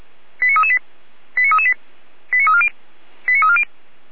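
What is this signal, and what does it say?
Five-tone selective call (ZVEI), heard as quick runs of five tones, tones of about 70 ms. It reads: 89338 89338 89340 89340
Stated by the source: ZVEI five-tone selective call 89327, received via SDR.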